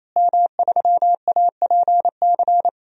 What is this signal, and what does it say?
Morse code, heard as a keyed tone, about 28 wpm, 700 Hz. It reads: M3APC
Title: DXCC-100 Call Signs - 5 Characters - Repeated 3x - 28wpm